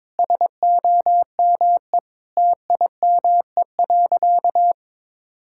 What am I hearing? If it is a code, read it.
Morse code, heard as a keyed tone, about 22 wpm, 700 Hz. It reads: SOME TIME.
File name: practice / 22wpm / Sentences with 3-Word Phrases - 22wpm